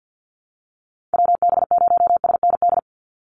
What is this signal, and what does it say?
Morse code, heard as a keyed tone, about 50 wpm, 700 Hz. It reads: WB0SND
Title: Call Signs Repeated 3x - 50wpm